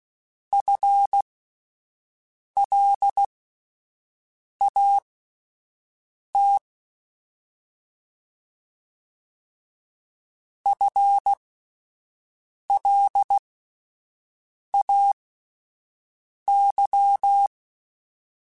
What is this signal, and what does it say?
Morse code, heard as a keyed tone, about 16 wpm, spaced as for 5 wpm, 800 Hz. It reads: FLAT FLAY
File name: Alphabetical / Alphabetical mp3s / Fs @5